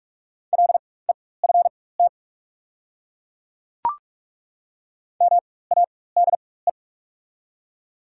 Morse code, heard as a keyed tone, about 42 wpm, spaced as for 21 wpm, 700 Hz. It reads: LEFT MADE
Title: 4x Character Spacing - Top 300 Words - 45wpm